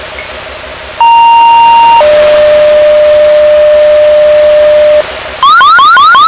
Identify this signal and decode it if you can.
Two-tone page, heard as a one-second tone, then a three-second tone, about 930 then 600 Hz, with noise. It is heard from 1.0 to 5.0 s.